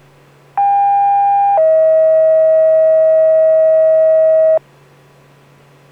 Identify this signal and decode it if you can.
Two-tone page, heard as a one-second tone, then a three-second tone, about 800 then 640 Hz. It is heard from 0.6 to 4.6 s.